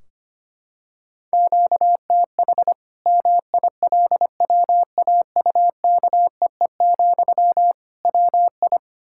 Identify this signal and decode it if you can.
Morse code, heard as a keyed tone, about 25 wpm, 700 Hz. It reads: QTH MILWAUKEE, WI
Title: Practice QSO - Exchange 1 - Beginning - 25wpm